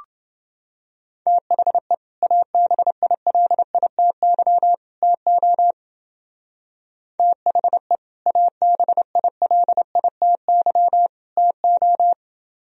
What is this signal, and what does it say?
Morse code, heard as a keyed tone, about 29 wpm, 700 Hz. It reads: THE ABILITY TO THE ABILITY TO